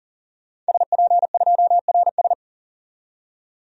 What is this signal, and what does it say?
Morse code, heard as a keyed tone, about 40 wpm, 700 Hz. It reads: SP2RS
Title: DXCC-100 Call Signs - 5 Characters - 40wpm